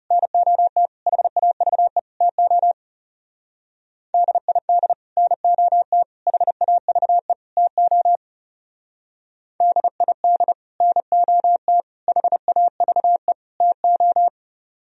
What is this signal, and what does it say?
Morse code, heard as a keyed tone, about 37 wpm, 700 Hz. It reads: NOT HAVE TO DID NOT HAVE TO DID NOT HAVE TO
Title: Speed-Racing - Sets of 4 Words 30wpm